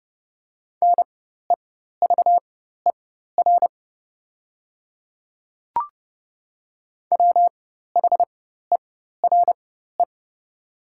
Morse code, heard as a keyed tone, about 30 wpm, spaced as for 14 wpm, 700 Hz. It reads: NEVER WHERE
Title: 4x Character Spacing - Top 300 Words - 30wpm